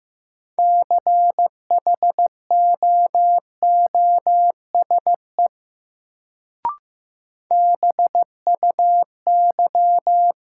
Morse code, heard as a keyed tone, about 15 wpm, 700 Hz. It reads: CHOOSE BUY
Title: Sentences from Top 700 Words - New Word Drill - 15wpm